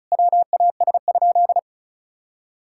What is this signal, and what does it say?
Morse code, heard as a keyed tone, about 36 wpm, 700 Hz. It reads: WAS?
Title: Sentences from Top 1000 Words - Review Part 6 - 35wpm